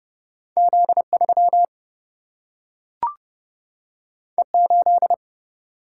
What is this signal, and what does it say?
Morse code, heard as a keyed tone, about 30 wpm, 700 Hz. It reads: Z3 E8
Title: Single-Letter Single-Number 30wpm